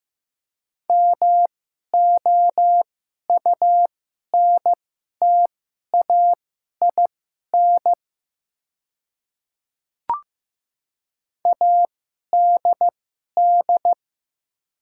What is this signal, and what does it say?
Morse code, heard as a keyed tone, about 15 wpm, spaced as for 11 wpm, 690 Hz. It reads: MOUNTAIN ADD
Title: 2x Character Spacing - Top 300 Words - 15wpm